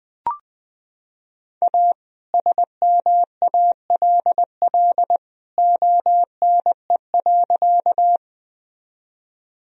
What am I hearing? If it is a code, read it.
Morse code, heard as a keyed tone, about 20 wpm, 700 Hz. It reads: A SMALL ONE.
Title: Sentences from Top 1000 Words - Review Part 1 - 20wpm